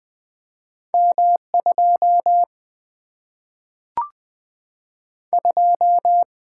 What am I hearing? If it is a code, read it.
Morse code, heard as a keyed tone, about 20 wpm, 700 Hz. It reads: M 2 2
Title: Single-Letter Single-Number 20wpm